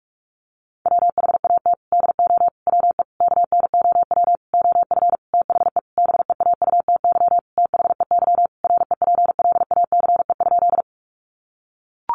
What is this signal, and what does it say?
Morse code, heard as a keyed tone, about 44 wpm, 700 Hz. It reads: WHAT DO WE KNOW OF THE BEAUTY THEY REPLACE?
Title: Sentences from Top 1400 Words 45wpm